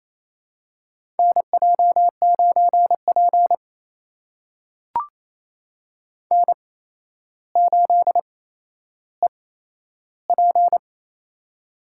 Morse code, heard as a keyed tone, about 28 wpm, spaced as for 12 wpm, 700 Hz. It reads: NJ9P N8EP
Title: USA Call Signs - 4 Characters - 8x Character Spacing - 28wpm